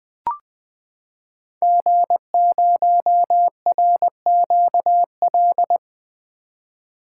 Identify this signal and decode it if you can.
Morse code, heard as a keyed tone, about 20 wpm, 700 Hz. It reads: G0RQL
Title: 2023 SOTA Chaser Call Signs - 20wpm